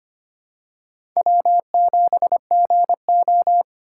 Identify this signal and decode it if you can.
Morse code, heard as a keyed tone, about 25 wpm, 700 Hz. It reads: W7GO
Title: USA Call Signs - 3x Character Spacing - 25wpm